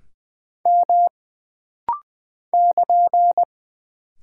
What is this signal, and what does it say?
Morse code, heard as a keyed tone, about 20 wpm, 700 Hz. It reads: M <KN>
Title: Single Character (Letter:Number:Punctuation:Prosign) - Rapid-Fire - 20wpm